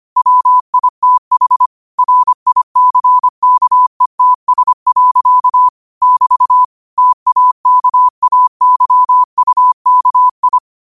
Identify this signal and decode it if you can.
Morse code, heard as a keyed tone, about 25 wpm, 1000 Hz. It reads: WITH RICKETS. <BT> TAKAYUKI
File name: qotdfiles / 25wpm